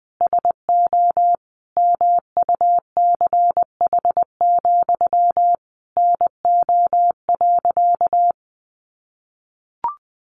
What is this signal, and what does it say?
Morse code, heard as a keyed tone, about 20 wpm, 700 Hz. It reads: SO MUCH, NO.
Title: A Sound Of Thunder 20wpm